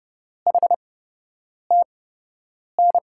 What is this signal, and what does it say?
Morse code, heard as a keyed tone, about 30 wpm, spaced as for 8 wpm, 700 Hz. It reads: HTN
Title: USA Call Signs - 8x Character Spacing - 30wpm